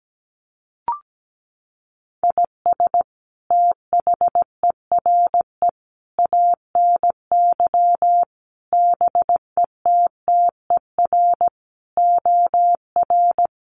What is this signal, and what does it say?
Morse code, heard as a keyed tone, about 17 wpm, 700 Hz. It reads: IS THERE ANY BETTER OR
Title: Sentences from Top 1400 Words 17wpm